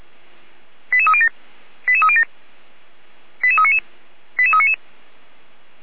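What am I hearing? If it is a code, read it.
Five-tone selective call (ZVEI), heard as quick runs of five tones, tones of about 70 ms. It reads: 88307 88307 88399 88399
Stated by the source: ZVEI five-tone call 88249 received by SDR, recorded by feuerwehr-orlamuende.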